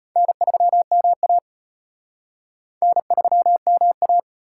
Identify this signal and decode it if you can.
Morse code, heard as a keyed tone, about 33 wpm, 700 Hz. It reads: N3MA N3MA